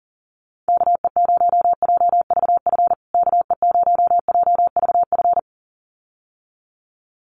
Morse code, heard as a keyed tone, about 37 wpm, 700 Hz. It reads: KE0JVF KE0JVF
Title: USA Call Signs Repeated 2x 40wpm